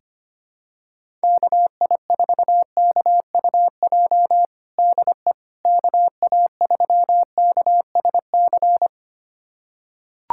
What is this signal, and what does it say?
Morse code, heard as a keyed tone, about 25 wpm, 700 Hz. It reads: KI4KUJ DE KA3KSC
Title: USA Callsign DE Callsign 25wpm